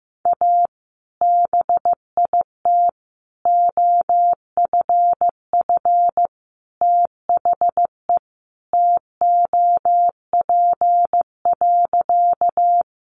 Morse code, heard as a keyed tone, about 15 wpm, 700 Hz. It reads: A BIT OFF THE TOP.
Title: Sentences from Top 900 Words 15wpm